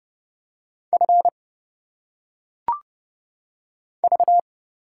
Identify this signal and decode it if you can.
Morse code, heard as a keyed tone, about 30 wpm, 700 Hz. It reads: F V
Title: Single Letters - Phonetic Pronunciation 30wpm